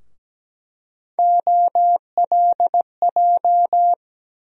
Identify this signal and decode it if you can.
Morse code, heard as a keyed tone, about 17 wpm, 700 Hz. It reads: OLJ
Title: USA Call Sign Postfixes - 3 Letters 17wpm